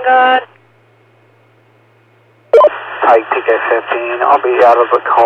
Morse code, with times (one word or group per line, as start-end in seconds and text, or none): none